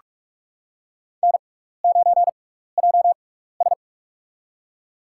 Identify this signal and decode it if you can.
Morse code, heard as a keyed tone, about 46 wpm, spaced as for 15 wpm, 700 Hz. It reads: N9JS
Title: USA Call Signs - 4 Characters - 6x Character Spacing - 45wpm